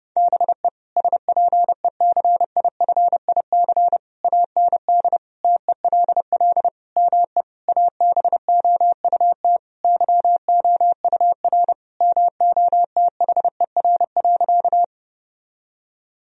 Morse code, heard as a keyed tone, about 30 wpm, 700 Hz. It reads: BE SPECIFIC AND TELL ME ABOUT YOUR MOTHER.
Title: Sentences from Top 1000 Words - Review Part 3 - 30wpm